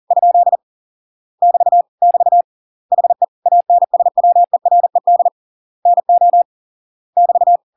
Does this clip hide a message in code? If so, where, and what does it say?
Yes: Morse code from 0.1 to 7.6 s, heard as a keyed tone, about 40 wpm, 700 Hz. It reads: ? <BT> <BT> HE ANSWERED NO <BT>